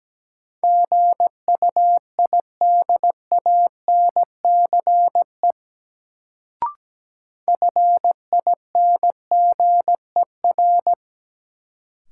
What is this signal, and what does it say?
Morse code, heard as a keyed tone, about 17 wpm, 700 Hz. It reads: GUIDANCE FINGER
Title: Sentences from Top 1600 Words - New Word Drill - 17wpm